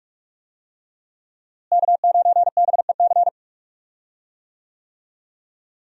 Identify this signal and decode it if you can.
Morse code, heard as a keyed tone, about 46 wpm, 700 Hz. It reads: K9BEC
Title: USA Call Signs - 5 Characters - Repeated 3x - 45wpm